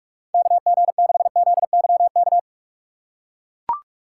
Morse code, heard as a keyed tone, about 42 wpm, 700 Hz. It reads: KG6ZYK
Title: USA Call Signs - 3x Character Spacing - 45wpm